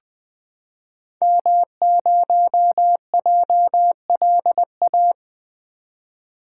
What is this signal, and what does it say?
Morse code, heard as a keyed tone, about 20 wpm, 700 Hz. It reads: M0JLA
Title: 2023 SOTA Activator Call Signs - 20wpm